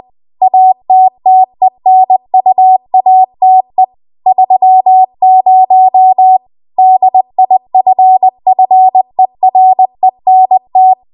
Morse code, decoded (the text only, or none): ATTENUATE 30 DIFFERENT